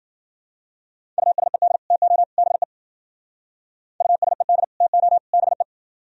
Morse code, heard as a keyed tone, about 49 wpm, 700 Hz. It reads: USED TO BE USED TO BE
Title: Speed-Racing - Sets of 3 Words 40wpm